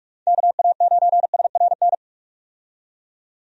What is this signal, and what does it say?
Morse code, heard as a keyed tone, about 46 wpm, 700 Hz. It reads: KA9SRN
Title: USA Call Signs - 6 Characters - 45wpm